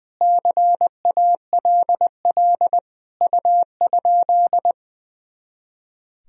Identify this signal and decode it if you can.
Morse code, heard as a keyed tone, about 21 wpm, 700 Hz. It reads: CALL U?
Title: Sentences from Top 100 Words and Abbreviations 20wpm